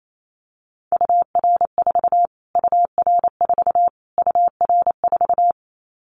Morse code, heard as a keyed tone, about 28 wpm, 700 Hz. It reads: UR4 UR4 UR4
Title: Callsign Prefixes for Top 25 DXCC - Repeated 3x - 28wpm